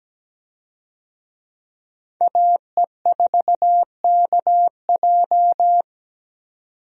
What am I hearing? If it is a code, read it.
Morse code, heard as a keyed tone, about 17 wpm, 700 Hz. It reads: AE4KJ
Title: USA Call Signs - 5 Characters - 17wpm